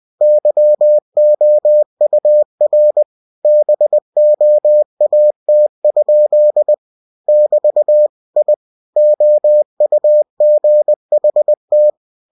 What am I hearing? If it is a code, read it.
Morse code, heard as a keyed tone, about 20 wpm, 600 Hz. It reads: YOUR BOAT? <BT> I OUGHT
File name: cw-books / winnie-the-pooh / winnie-20wpm-page-0099